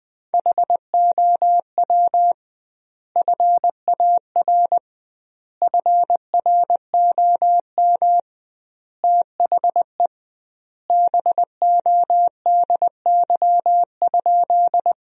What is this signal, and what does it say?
Morse code, heard as a keyed tone, about 21 wpm, 700 Hz. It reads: HOW FAR FROM THE BODY?
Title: 2x Word-Spacing - Sentences from Top 400 Words - 20wpm